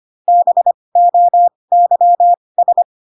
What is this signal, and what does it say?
Morse code, heard as a keyed tone, about 25 wpm, spaced as for 20 wpm, 700 Hz. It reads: BOYS
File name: Pinocchio / Pinocchio ch8 15 25 ik1hge